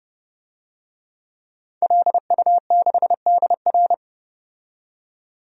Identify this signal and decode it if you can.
Morse code, heard as a keyed tone, about 29 wpm, 700 Hz. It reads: LU6DR
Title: DXCC-100 Call Signs - 5 Characters - 30wpm